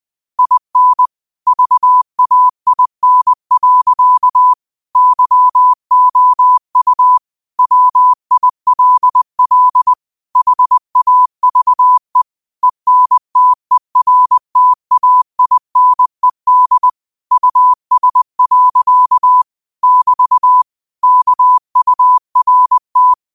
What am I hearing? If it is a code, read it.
Morse code, heard as a keyed tone, about 20 wpm, 1000 Hz. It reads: IN VAIN. YOU WILL HAVE ENTERTAINED US. <BT> KURT
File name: qotdfiles / 20wpm